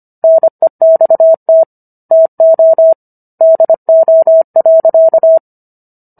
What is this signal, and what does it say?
Morse code, heard as a keyed tone, about 25 wpm, 650 Hz. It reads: NEXT TO DO.